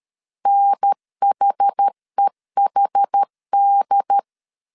Morse code, heard as a keyed tone, about 13 wpm, 800 Hz. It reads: NHEHD
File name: CW / DET / 1 3 1500C S C 12wpm 2750